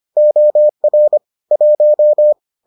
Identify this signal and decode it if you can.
Morse code, heard as a keyed tone, about 25 wpm, 600 Hz. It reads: OR 1